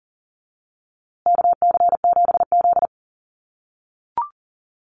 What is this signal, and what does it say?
Morse code, heard as a keyed tone, about 40 wpm, 700 Hz. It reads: KC7Z
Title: USA Call Signs Repeated 3x 40wpm